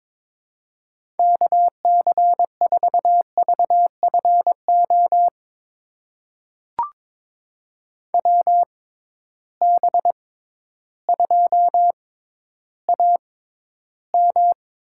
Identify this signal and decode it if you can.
Morse code, heard as a keyed tone, about 22 wpm, spaced as for 12 wpm, 700 Hz. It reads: KC4VFO WB2AM
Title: USA Call Signs - 6 Characters - 6x Character Spacing - 22wpm